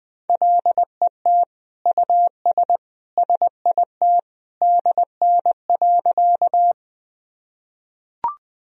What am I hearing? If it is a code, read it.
Morse code, heard as a keyed tone, about 20 wpm, 700 Hz. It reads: LET US SIT DN.
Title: Sentences from Top 500 Words and Abbreviations 20wpm